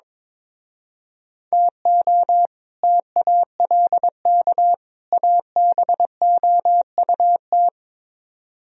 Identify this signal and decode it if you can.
Morse code, heard as a keyed tone, about 24 wpm, 700 Hz. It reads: TO TALK ABOUT
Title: Speed-Racing - Sets of 3 Words 22wpm